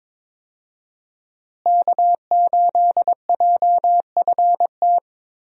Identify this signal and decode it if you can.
Morse code, heard as a keyed tone, about 22 wpm, 700 Hz. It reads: K8JFT
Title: USA Call Signs - 5 Characters - Repeated 2x - 22wpm